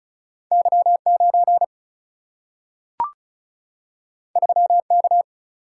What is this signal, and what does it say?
Morse code, heard as a keyed tone, about 36 wpm, 700 Hz. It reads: Y 9 3 K
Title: Single-Letter Single-Number 35wpm